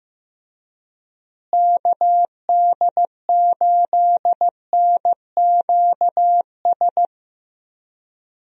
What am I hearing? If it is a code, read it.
Morse code, heard as a keyed tone, about 15 wpm, 700 Hz. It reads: KD8NQS
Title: USA Call Signs - 6 Characters - Repeated 3x - 15wpm